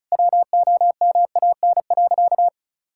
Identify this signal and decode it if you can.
Morse code, heard as a keyed tone, about 34 wpm, 700 Hz. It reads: WOMAN.
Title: Sentences from Top 1600 Words 35wpm